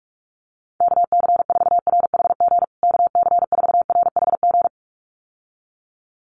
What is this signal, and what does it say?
Morse code, heard as a keyed tone, about 44 wpm, 700 Hz. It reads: KC4RHG KC4RHG
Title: USA Call Signs Repeated 2x 45wpm